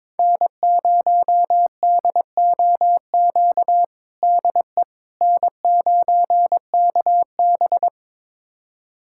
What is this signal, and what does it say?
Morse code, heard as a keyed tone, about 22 wpm, 700 Hz. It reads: N0DOQ DE N9KB
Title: USA Callsign DE Callsign 22wpm